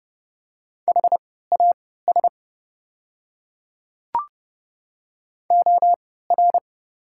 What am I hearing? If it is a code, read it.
Morse code, heard as a keyed tone, about 30 wpm, spaced as for 17 wpm, 700 Hz. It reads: HAS OR
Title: 3x Character Spacing - Top 300 Words - 30wpm